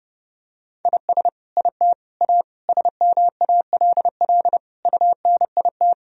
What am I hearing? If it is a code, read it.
Morse code, heard as a keyed tone, about 30 wpm, 700 Hz. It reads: IS IT A SMALL UNIT
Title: Sentences from Top 1000 Words - Review Part 2 - 30wpm